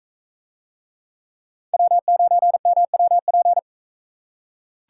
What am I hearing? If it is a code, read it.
Morse code, heard as a keyed tone, about 40 wpm, 700 Hz. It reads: W9MWP